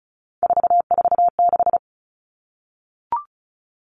Morse code, heard as a keyed tone, about 34 wpm, 700 Hz. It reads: 446
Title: Numbers - 3-Digit 35wpm